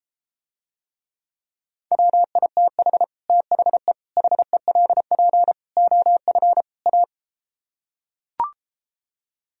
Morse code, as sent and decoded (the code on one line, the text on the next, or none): .-- .. - .... / - .... / .... .-.. .--. / --- ..-. / .-
WITH THE HELP OF A